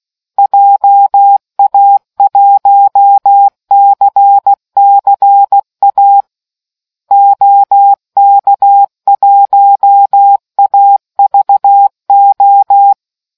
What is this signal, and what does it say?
Morse code, heard as a keyed tone, about 16 wpm, 800 Hz. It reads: JA1CCA OK1AVO